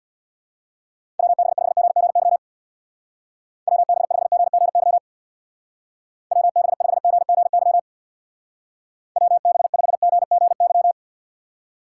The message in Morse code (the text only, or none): WB5GGY WB5GGY WB5GGY WB5GGY